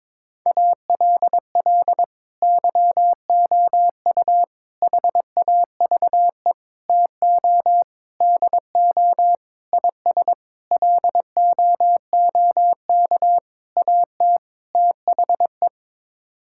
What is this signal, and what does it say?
Morse code, heard as a keyed tone, about 22 wpm, 700 Hz. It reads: ALL YOU HAVE TO DO IS LOOK AT THE
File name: Sets of 9 Words - 22wpm